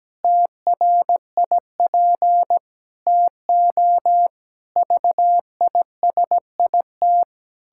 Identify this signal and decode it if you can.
Morse code, heard as a keyed tone, about 17 wpm, 700 Hz. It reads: TRIP TO VISIT
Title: Pangram Challenge v2 - 17wpm